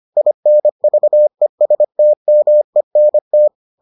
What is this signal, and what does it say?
Morse code, heard as a keyed tone, about 25 wpm, 600 Hz. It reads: INVESTMENT